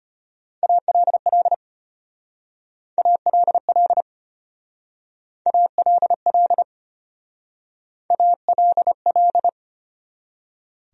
Morse code, heard as a keyed tone, about 30 wpm, 700 Hz. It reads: ALL ALL ALL ALL
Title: Speed-Racing Top 100 Words 25wpm